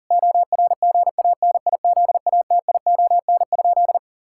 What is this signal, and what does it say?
Morse code, heard as a keyed tone, about 37 wpm, 700 Hz. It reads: ORGANIZATION?
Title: Sentences from Top 1000 Words - Review Part 5 - 40wpm